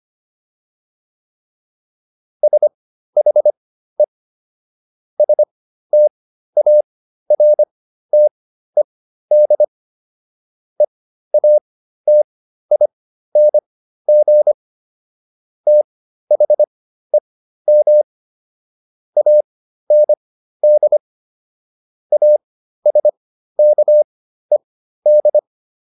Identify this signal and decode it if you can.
Morse code, heard as a keyed tone, about 25 wpm, spaced as for 13 wpm, 600 Hz. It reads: SHE STARTED EATING THEM AND ASKED